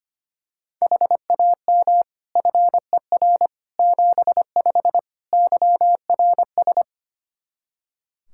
Morse code, heard as a keyed tone, about 25 wpm, 700 Hz. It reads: HAM FER 75 YRS
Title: Practice QSO - Exchange 3 - More Info - 25wpm